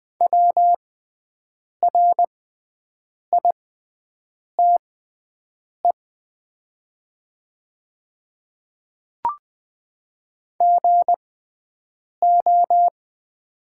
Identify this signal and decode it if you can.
Morse code, heard as a keyed tone, about 20 wpm, spaced as for 7 wpm, 700 Hz. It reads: WRITE GO
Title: 6x Character Spacing - Top 100 Words - 20wpm